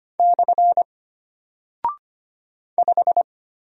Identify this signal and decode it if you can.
Morse code, heard as a keyed tone, about 25 wpm, 700 Hz. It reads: / 5
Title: Single Character (Letter:Number:Punctuation:Prosign) Alt 25wpm